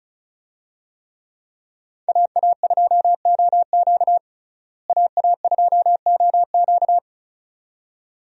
Morse code, heard as a keyed tone, about 36 wpm, 700 Hz. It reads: AA2OQ AA2OQ
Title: USA Call Signs - 5 Characters - Repeated 2x - 35wpm